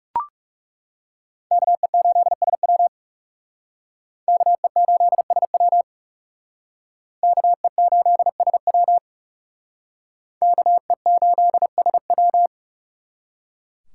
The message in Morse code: KE8SW KE8SW KE8SW KE8SW

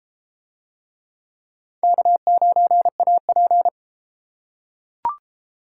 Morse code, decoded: K9AP

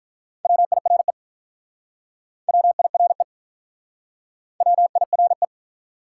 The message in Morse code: WIRE WIRE WIRE

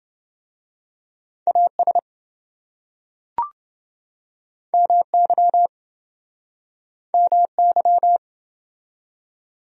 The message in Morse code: AS MY MY

